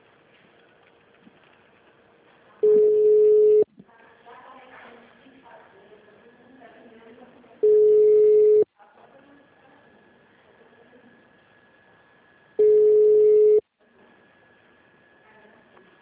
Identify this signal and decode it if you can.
Ringback tone (Europe, 425 Hz), with speech.
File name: out- -2004- - - .1333